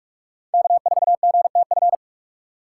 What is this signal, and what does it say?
Morse code, heard as a keyed tone, about 46 wpm, 700 Hz. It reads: K4GTF
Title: USA Call Signs - 5 Characters - 3x Character Spacing - 45wpm